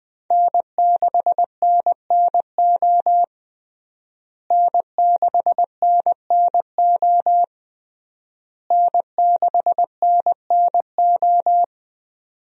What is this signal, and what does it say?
Morse code, heard as a keyed tone, about 20 wpm, 700 Hz. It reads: N6NNO N6NNO N6NNO
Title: USA Call Signs - 5 Characters - Repeated 3x - 20wpm